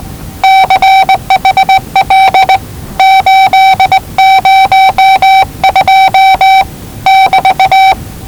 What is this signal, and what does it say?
Morse code, heard as a keyed tone, about 19 wpm, 770 Hz. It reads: CHL 802 <BT>
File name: WellingtonRadioZlwCloseDownOnHfAndMf / 01-Zlw 4277khz 417.5 qrt until 0925z